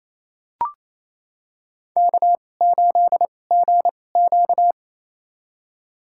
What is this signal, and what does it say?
Morse code, heard as a keyed tone, about 28 wpm, spaced as for 20 wpm, 700 Hz. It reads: K8GQ
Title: USA Call Signs - 4 Characters - 2x Character Spacing - 28wpm